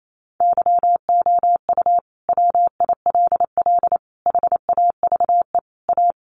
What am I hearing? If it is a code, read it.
Morse code, heard as a keyed tone, about 28 wpm, 700 Hz. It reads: YOU WILL HAVE A
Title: Sentences from Top 1100 Words 28wpm